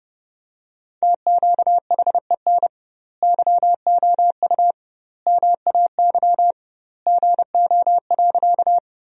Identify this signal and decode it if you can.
Morse code, heard as a keyed tone, about 32 wpm, 700 Hz. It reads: TQHEN YOU MAY GO.
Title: 2x Word-Spacing - Sentences from Top 100 Words - 30wpm